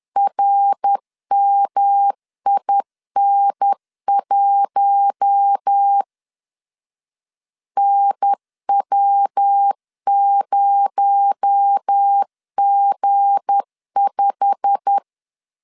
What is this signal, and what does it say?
Morse code, heard as a keyed tone, about 10 wpm, 800 Hz. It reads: RMIN1 NW0G5
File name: LEZIONI / 8 YJ45 / 8 YJ45 10wpm 2549